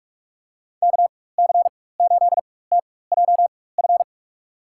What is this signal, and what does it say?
Morse code, heard as a keyed tone, about 44 wpm, spaced as for 21 wpm, 700 Hz. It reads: KC8TJF
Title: USA Call Signs - 6 Characters - 4x Character Spacing - 45wpm